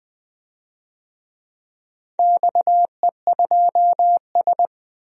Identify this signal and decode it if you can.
Morse code, heard as a keyed tone, about 20 wpm, 700 Hz. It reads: XE2S
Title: DXCC-100 Call Signs - 4 Characters - Repeated 2x - 20wpm